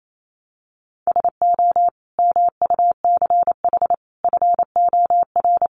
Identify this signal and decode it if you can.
Morse code, heard as a keyed tone, about 28 wpm, 700 Hz. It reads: SO MUCH FOR
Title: Sets of 5 Words 28wpm